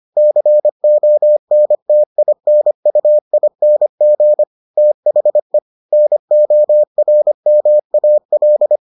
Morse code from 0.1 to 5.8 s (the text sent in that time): CONTINUING THE